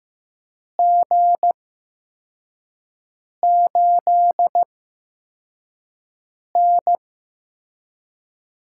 Morse code, heard as a keyed tone, about 15 wpm, spaced as for 4 wpm, 700 Hz. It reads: G8N
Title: USA Call Signs - 4 Characters - 8x Character Spacing - 15wpm